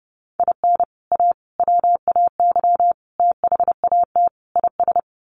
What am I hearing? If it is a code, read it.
Morse code, heard as a keyed tone, about 30 wpm, 700 Hz. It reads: IN A WAY THAT IS
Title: Sets of 5 Words Encore 30wpm